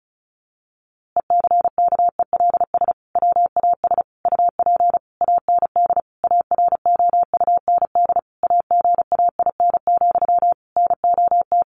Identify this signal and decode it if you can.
Morse code, heard as a keyed tone, about 34 wpm, 700 Hz. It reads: ECKELS WAS UP AND AROUND AGAIN, NOT
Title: A Sound Of Thunder 35wpm